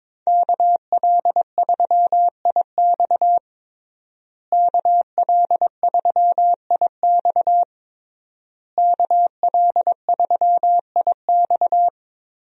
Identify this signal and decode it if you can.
Morse code, heard as a keyed tone, about 22 wpm, 700 Hz. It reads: KL3IX KL3IX KL3IX
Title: USA Call Signs - 5 Characters - Repeated 3x - 22wpm